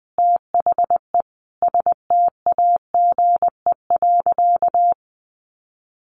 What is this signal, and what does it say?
Morse code, heard as a keyed tone, about 20 wpm, 700 Hz. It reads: THE STAGE.
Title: Sentences from Top 1500 Words 20wpm